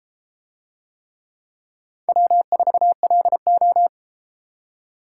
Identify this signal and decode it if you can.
Morse code, heard as a keyed tone, about 32 wpm, 700 Hz. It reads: W4LO